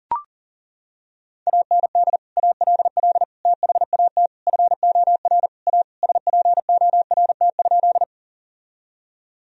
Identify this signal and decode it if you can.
Morse code, heard as a keyed tone, about 37 wpm, 700 Hz. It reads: AND ALL THAT FOR A SPORT?
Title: Sentences from Top 1000 Words - Review Part 4 - 40wpm